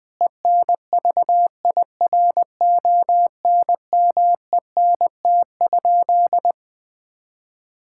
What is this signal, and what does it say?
Morse code, heard as a keyed tone, about 20 wpm, 700 Hz. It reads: ENVIRONMENT?
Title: Sentences from Top 1000 Words - Review Part 5 - 20wpm